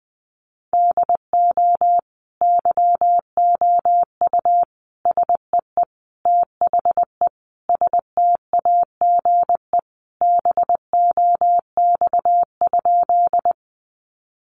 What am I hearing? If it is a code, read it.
Morse code, heard as a keyed tone, about 20 wpm, 700 Hz. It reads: DO YOU SEE THE STAGE BOX?
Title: Sentences from Top 1200 Words 20wpm